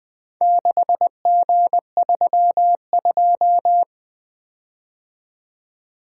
Morse code, heard as a keyed tone, about 20 wpm, 700 Hz. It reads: 6G32
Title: Four Letter-Number Combinations - 20wpm